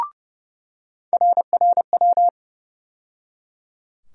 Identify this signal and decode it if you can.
Morse code, heard as a keyed tone, about 30 wpm, 700 Hz. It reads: RRW
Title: USA Call Sign Postfixes - 3 Letters 30wpm